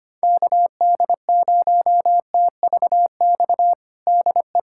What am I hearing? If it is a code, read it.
Morse code, heard as a keyed tone, about 25 wpm, 700 Hz. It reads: KD0TVX DE
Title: USA Callsign DE Callsign 25wpm